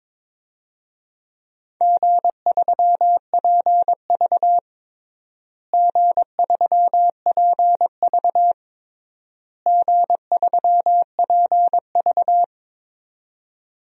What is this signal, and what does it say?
Morse code, heard as a keyed tone, about 22 wpm, 700 Hz. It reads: G3PV G3PV G3PV
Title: DXCC-100 Call Signs - 4 Characters - Repeated 3x - 22wpm